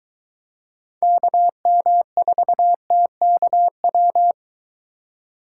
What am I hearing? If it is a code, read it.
Morse code, heard as a keyed tone, about 22 wpm, 700 Hz. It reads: KM4TKW